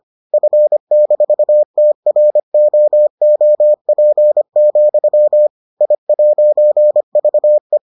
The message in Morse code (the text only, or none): F-TROOP, I'VE